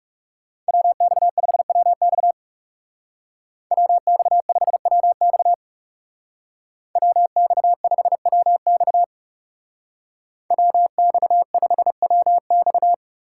Morse code, heard as a keyed tone, about 37 wpm, 700 Hz. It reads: WX5WX WX5WX WX5WX WX5WX
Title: Speed Racing - USA Call Signs 30wpm